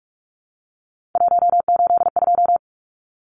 Morse code, heard as a keyed tone, about 44 wpm, 700 Hz. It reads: 182
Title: Numbers - 3-Digit 45wpm